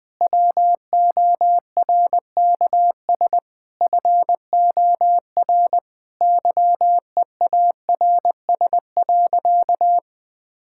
Morse code, heard as a keyed tone, about 20 wpm, 700 Hz. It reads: WORKS FOR YEARS.
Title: Sentences with 6-Word Phrases - 20wpm